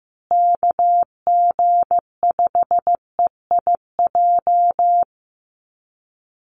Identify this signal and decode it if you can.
Morse code, heard as a keyed tone, about 15 wpm, 700 Hz. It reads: KG5EIJ
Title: USA Call Signs Repeated 3x 15wpm